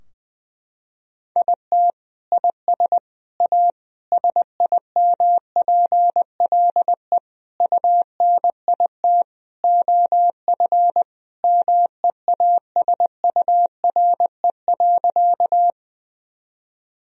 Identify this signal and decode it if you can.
Morse code, heard as a keyed tone, about 21 wpm, 700 Hz. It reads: IT IS A SIMPLE UNIT OF MEASURE.
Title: Sentences from Top 1000 Words - Review Part 2 - 20wpm